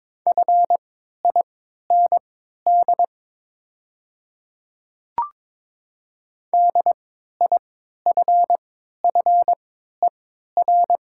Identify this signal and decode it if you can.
Morse code, heard as a keyed tone, about 22 wpm, spaced as for 12 wpm, 700 Hz. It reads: FIND DIFFER
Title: 3x Character Spacing - Top 200 Words - 22wpm